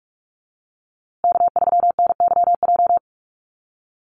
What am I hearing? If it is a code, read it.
Morse code, heard as a keyed tone, about 44 wpm, 700 Hz. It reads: K3NYJ
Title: USA Call Signs Repeated 3x 45wpm